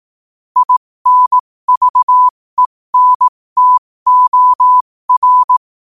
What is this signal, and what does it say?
Morse code, heard as a keyed tone, about 18 wpm, spaced as for 15 wpm, 1000 Hz. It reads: INVENTOR
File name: qotdfiles / 15wpm